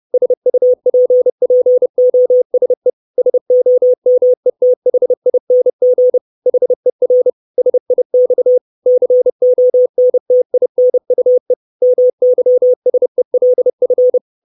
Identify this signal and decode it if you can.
Morse code, heard as a keyed tone, about 30 wpm, 500 Hz. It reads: SUPPOSE SOMETHING HER SIX CONTINUE MYSELF
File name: 632 Words 30WPM